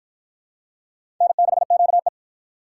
Morse code, heard as a keyed tone, about 47 wpm, 700 Hz. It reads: N6QE